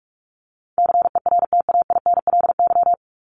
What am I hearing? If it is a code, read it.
Morse code, heard as a keyed tone, about 44 wpm, 700 Hz. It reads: CERTAINLY
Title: LY Suffix - Top 100 Words with Base Words 45wpm